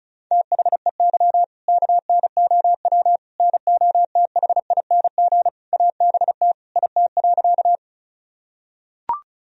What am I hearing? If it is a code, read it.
Morse code, heard as a keyed tone, about 34 wpm, 700 Hz. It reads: THEY KNOW NOTHING ABT IT.
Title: Sentences from Top 500 Words and Abbreviations 35wpm